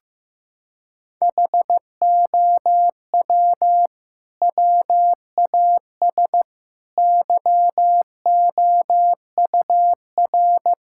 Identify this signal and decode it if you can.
Morse code, heard as a keyed tone, about 16 wpm, 700 Hz. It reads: HOW WAS YOUR
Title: Sentences from Top 1600 Words 15wpm